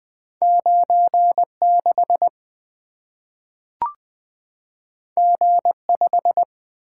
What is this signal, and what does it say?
Morse code, heard as a keyed tone, about 20 wpm, 700 Hz. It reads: 9 6 G 5
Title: Two Letter-Number Combinations - 20wpm